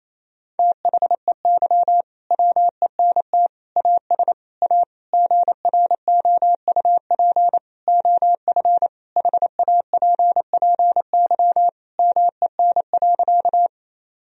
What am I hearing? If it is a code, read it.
Morse code, heard as a keyed tone, about 28 wpm, 700 Hz. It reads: THEY WENT AS A GROUP OF HAPPY MEN.
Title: Sentences from Top 1000 Words - Review Part 1 - 28wpm